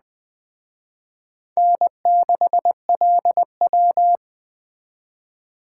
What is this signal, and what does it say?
Morse code, heard as a keyed tone, about 20 wpm, 700 Hz. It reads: N6LW
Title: Speed Racing - USA 1x2 Call Signs 17wpm